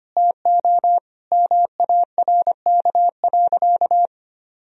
Morse code, heard as a keyed tone, about 25 wpm, 700 Hz. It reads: TO MARK.
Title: Sentences from Top 1000 Words - Review Part 2 - 25wpm